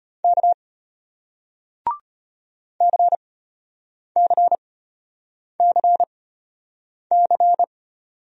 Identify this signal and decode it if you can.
Morse code, heard as a keyed tone, about 34 wpm, spaced as for 8 wpm, 700 Hz. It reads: K CCCC